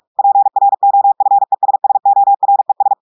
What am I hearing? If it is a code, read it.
Morse code, heard as a keyed tone, about 46 wpm, 800 Hz. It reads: PROFESSORES
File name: portugais texte en clair 1 vitesse 45 mots minutes